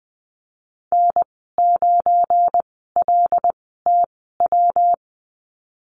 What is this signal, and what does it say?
Morse code, heard as a keyed tone, about 20 wpm, spaced as for 15 wpm, 700 Hz. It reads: N9LTW
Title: 179 Lesson 29g - Slash - Call Signs 20wpm